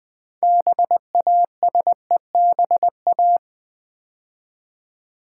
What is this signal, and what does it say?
Morse code, heard as a keyed tone, about 20 wpm, 700 Hz. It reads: BASEBA
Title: Build A Word - 2 Letters At A Time - 8-Letter Words - 20wpm